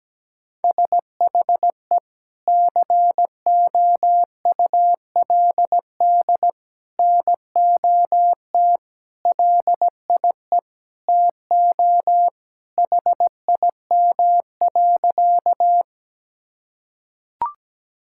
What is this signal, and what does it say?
Morse code, heard as a keyed tone, about 17 wpm, 700 Hz. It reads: SHE COULD NOT LIE TO HIM.
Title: Sentences from Top 1600 Words 17wpm